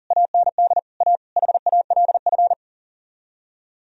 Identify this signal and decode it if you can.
Morse code, heard as a keyed tone, about 37 wpm, 700 Hz. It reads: AND A HALF
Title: Speed-Racing - Sets of 5 Words 40wpm